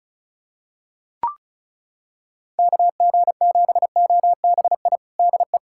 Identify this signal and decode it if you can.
Morse code, heard as a keyed tone, about 36 wpm, 700 Hz. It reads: KG7OBI DE
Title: USA Callsign DE Callsign - 6 Characters - 35wpm